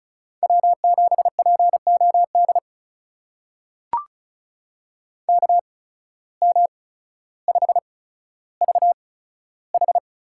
Morse code, heard as a keyed tone, about 34 wpm, spaced as for 15 wpm, 700 Hz. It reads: W7POD KM5VH
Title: USA Call Signs - 5 Characters - 8x Character Spacing - 35wpm